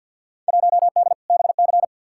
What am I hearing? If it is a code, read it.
Morse code, heard as a keyed tone, about 47 wpm, 700 Hz. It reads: 1D BC
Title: Field Day Exchanges - 50wpm